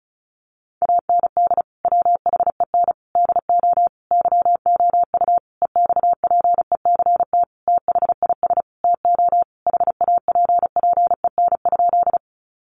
Morse code, heard as a keyed tone, about 34 wpm, 700 Hz. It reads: AND WHEN DO YOU EXPECT THIS TO HAPPEN?
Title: Sentences from Top 1100 Words 35wpm